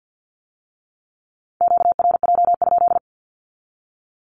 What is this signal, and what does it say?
Morse code, heard as a keyed tone, about 50 wpm, 700 Hz. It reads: QRJ?
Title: Q-Codes with Spoken Meaning - 50wpm